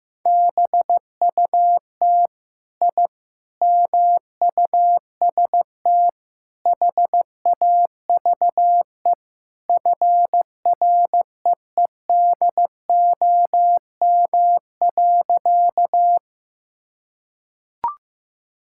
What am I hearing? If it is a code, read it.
Morse code, heard as a keyed tone, about 15 wpm, 700 Hz. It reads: BUT I MUST HAVE FREEDOM.
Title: Sentences from Top 1000 Words - Review Part 5 - 15wpm